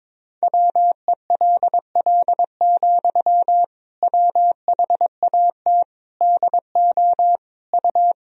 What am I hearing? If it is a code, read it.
Morse code, heard as a keyed tone, about 22 wpm, 700 Hz. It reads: WELL, WHAT DO U
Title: Sentences from Top 200 Words and Abbreviations 22wpm